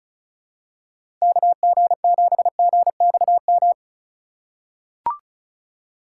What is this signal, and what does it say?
Morse code, heard as a keyed tone, about 36 wpm, 700 Hz. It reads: KG7GXM
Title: USA Call Signs - 6 Characters - 3x Character Spacing - 35wpm